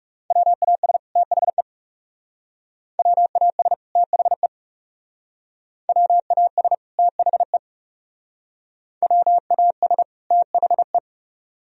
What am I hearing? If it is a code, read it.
Morse code, heard as a keyed tone, about 39 wpm, 700 Hz. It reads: WAS THE WAS THE WAS THE WAS THE